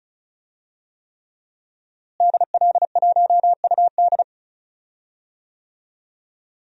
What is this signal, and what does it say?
Morse code, heard as a keyed tone, about 36 wpm, 700 Hz. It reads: DL1UD
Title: DXCC-100 Call Signs - 5 Characters - Repeated 3x - 35wpm